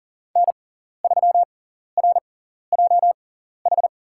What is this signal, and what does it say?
Morse code, heard as a keyed tone, about 37 wpm, spaced as for 14 wpm, 700 Hz. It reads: N3RJH
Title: USA Call Signs - 5 Characters - 6x Character Spacing - 40wpm